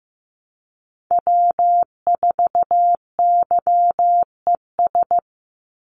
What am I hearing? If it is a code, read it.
Morse code, heard as a keyed tone, about 15 wpm, 700 Hz. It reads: W4YES
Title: Call Signs Repeated 3x - 15wpm